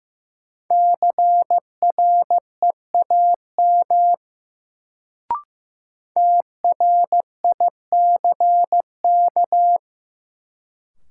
Morse code, heard as a keyed tone, about 15 wpm, 700 Hz. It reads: CREAM TRICK
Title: Sentences from Top 1500 Words - New Word Drill - 15wpm